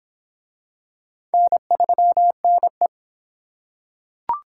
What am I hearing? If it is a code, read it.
Morse code, heard as a keyed tone, about 25 wpm, 700 Hz. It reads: N3NE